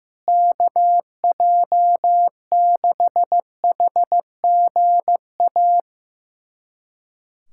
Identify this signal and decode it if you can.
Morse code, heard as a keyed tone, about 15 wpm, 700 Hz. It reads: KJ6HGA
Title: USA Callsign DE Callsign - 6 Characters - 15wpm